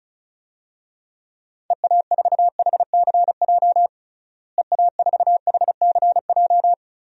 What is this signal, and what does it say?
Morse code, heard as a keyed tone, about 34 wpm, 700 Hz. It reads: EA4HCJ EA4HCJ
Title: DXCC-100 Call Signs - 6 Characters - Repeated 2x - 35wpm